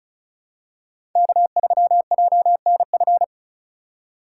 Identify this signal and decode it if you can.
Morse code, heard as a keyed tone, about 34 wpm, 700 Hz. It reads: K3JNF
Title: USA Call Signs - 5 Characters - 4x Character Spacing - 35wpm